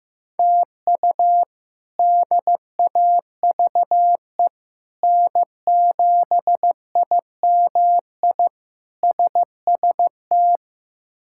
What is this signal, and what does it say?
Morse code, heard as a keyed tone, about 15 wpm, 700 Hz. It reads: TU DAVE N7IMI SST
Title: K1USN SST QSO Practice - 15wpm